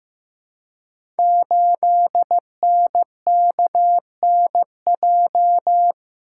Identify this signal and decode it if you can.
Morse code, heard as a keyed tone, about 15 wpm, 700 Hz. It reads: 8NKNJ
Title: Code Groups - 5 Letter:Numbers - 15wpm